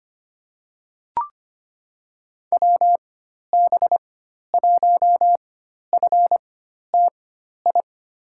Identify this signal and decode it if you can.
Morse code, heard as a keyed tone, about 25 wpm, spaced as for 12 wpm, 700 Hz. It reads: WB1FTI
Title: USA Call Signs - 6 Characters - 4x Character Spacing - 25wpm